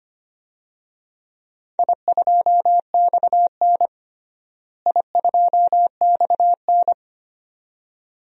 Morse code, heard as a keyed tone, about 25 wpm, 700 Hz. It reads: I2XN I2XN
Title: DXCC-100 Call Signs - 4 Characters - Repeated 2x - 25wpm